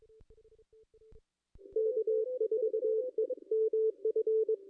Morse code, heard as a keyed tone, about 22 wpm, 440 Hz. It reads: K4EMF
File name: contests / ss-cw-k5tr-op-n5rz -2200